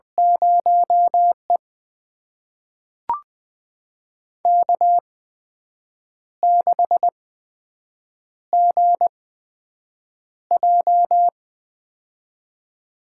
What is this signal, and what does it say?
Morse code, heard as a keyed tone, about 20 wpm, spaced as for 7 wpm, 700 Hz. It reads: 0E K6GJ
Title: USA Call Signs - 4 Characters - 8x Character Spacing - 20wpm